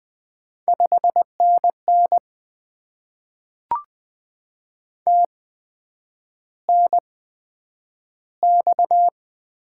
Morse code, heard as a keyed tone, about 20 wpm, spaced as for 9 wpm, 700 Hz. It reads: 5NN TNX
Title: Top 20 QSO Elements - 8x Character Spacing - 20wpm